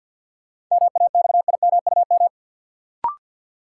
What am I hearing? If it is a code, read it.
Morse code, heard as a keyed tone, about 52 wpm, 700 Hz. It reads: MAXIMUM
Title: Sentences from Top 1300 Words - New Word Drill Alt - 50wpm